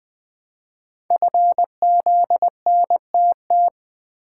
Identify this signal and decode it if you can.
Morse code, heard as a keyed tone, about 20 wpm, 700 Hz. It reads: FZNTT
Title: Code Groups - 5 Letters - 20wpm